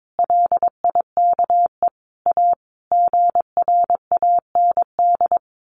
Morse code, heard as a keyed tone, about 22 wpm, 700 Hz. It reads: LIKE A GRAND
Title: A Sound Of Thunder 22wpm